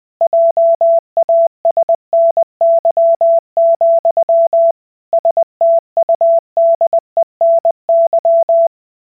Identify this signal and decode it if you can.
Morse code, heard as a keyed tone, about 20 wpm, 670 Hz. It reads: JASNY, STUDENY